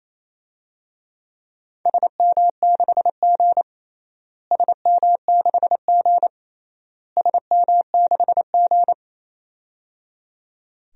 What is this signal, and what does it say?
Morse code, heard as a keyed tone, about 28 wpm, 700 Hz. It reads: SM6G SM6G SM6G
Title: DXCC-100 Call Signs - 4 Characters - Repeated 3x - 28wpm